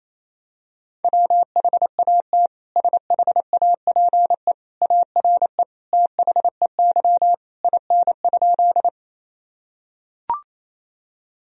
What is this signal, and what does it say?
Morse code, heard as a keyed tone, about 28 wpm, 700 Hz. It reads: WHAT SHAPE ARE THEY IN?
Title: Sentences from Top 1000 Words - Review Part 3 - 28wpm